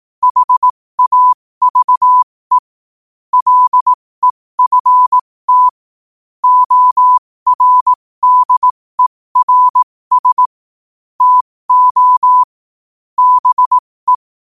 Morse code, heard as a keyed tone, about 18 wpm, spaced as for 15 wpm, 1000 Hz. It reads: HAVE LEFT ORDERS TO BE
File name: qotdfiles / 15wpm